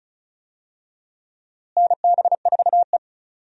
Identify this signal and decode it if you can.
Morse code, heard as a keyed tone, about 34 wpm, 700 Hz. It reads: NB4E
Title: USA Call Signs - 4 Characters - Repeated 2x - 35wpm